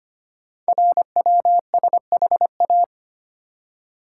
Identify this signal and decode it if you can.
Morse code, heard as a keyed tone, about 25 wpm, 700 Hz. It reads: RWSHA
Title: Code Groups - 5 Letters - 25wpm